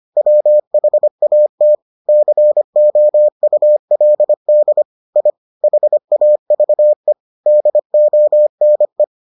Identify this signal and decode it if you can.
Morse code, heard as a keyed tone, about 25 wpm, 600 Hz. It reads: WHAT COULD I HAVE DONE